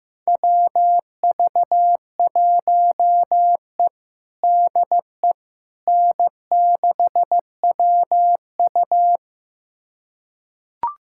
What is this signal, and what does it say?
Morse code, heard as a keyed tone, about 16 wpm, 700 Hz. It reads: WV1E DE N6WU
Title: USA Callsign DE Callsign - 4 Characters - 15wpm